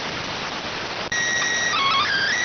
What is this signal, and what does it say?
SSTV (slow-scan television), PD 290.